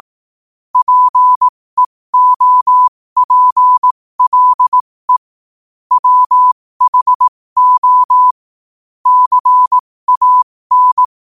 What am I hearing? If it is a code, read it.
Morse code, heard as a keyed tone, about 18 wpm, spaced as for 15 wpm, 1000 Hz. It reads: PEOPLE WHO CAN
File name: qotdfiles / 15wpm